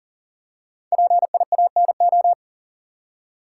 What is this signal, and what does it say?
Morse code, heard as a keyed tone, about 40 wpm, 700 Hz. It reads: PIANO
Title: Five Letter Words - 40wpm